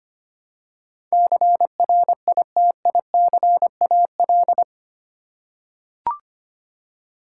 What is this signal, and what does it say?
Morse code, heard as a keyed tone, about 25 wpm, 700 Hz. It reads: CRITICAL
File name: 8 Letter Words - First 3 Letters Given - 25wpm